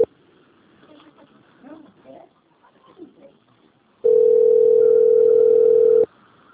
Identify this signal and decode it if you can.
Ringback tone (North America), 440 + 480 Hz.
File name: out- - - - - .320